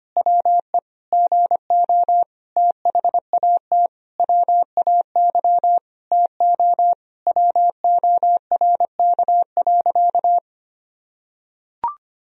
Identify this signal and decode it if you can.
Morse code, heard as a keyed tone, about 25 wpm, 700 Hz. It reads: WE GO THAT WAY TO WORK.
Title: Sentences from Top 1000 Words - Review Part 3 - 25wpm